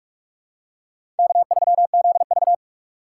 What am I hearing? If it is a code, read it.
Morse code, heard as a keyed tone, about 45 wpm, 700 Hz. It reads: K3ZV
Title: USA Call Signs - 4 Characters - 2x Character Spacing - 45wpm